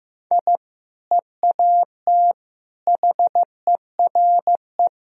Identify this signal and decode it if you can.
Morse code, heard as a keyed tone, about 15 wpm, 700 Hz. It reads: I EAT HERE
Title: Sentences from Top 1000 Words - Review Part 3 - 15wpm